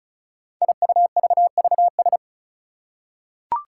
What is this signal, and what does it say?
Morse code, heard as a keyed tone, about 36 wpm, 700 Hz. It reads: IUVVS
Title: Five Challenging Letter-Number Combinations 35wpm